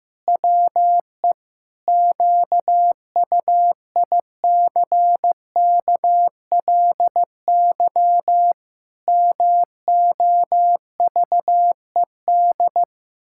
Morse code, heard as a keyed tone, about 15 wpm, 700 Hz. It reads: WE QUICKLY MOVED
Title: Pangram Challenge v2 - 15wpm